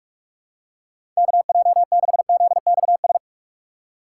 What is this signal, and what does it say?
Morse code, heard as a keyed tone, about 47 wpm, 700 Hz. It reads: KJ6ZXS
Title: USA Call Signs - 6 Characters - 2x Character Spacing - 45wpm